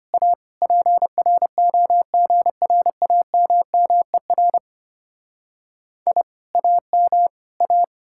Morse code, heard as a keyed tone, about 31 wpm, 700 Hz. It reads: A PROGRAMMER I AM A
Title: Speed-Racing Common QSO Phrases - 25wpm